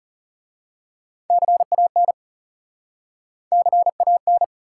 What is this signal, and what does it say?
Morse code, heard as a keyed tone, about 40 wpm, 700 Hz. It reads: CAN CAN